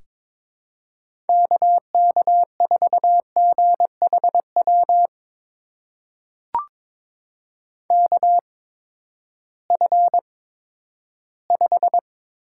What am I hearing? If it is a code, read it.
Morse code, heard as a keyed tone, about 22 wpm, spaced as for 12 wpm, 700 Hz. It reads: KK4GHW KF5
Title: USA Call Signs - 6 Characters - 8x Character Spacing - 22wpm